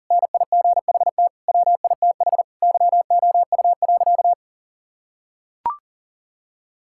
Morse code, heard as a keyed tone, about 40 wpm, 690 Hz. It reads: NIGHT WITH YOU.
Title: Sentences from Top 1000 Words - Review Part 5 - 40wpm